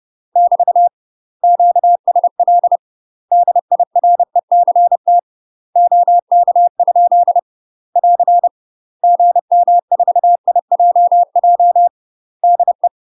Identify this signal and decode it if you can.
Morse code, heard as a keyed tone, about 30 wpm, 700 Hz. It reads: <BT> QSL DIRECT OK? <AR> GM4IJJ DE
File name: UK MASTER QSOs TRAINING / 30WPM / UK MASTER QSOs 0015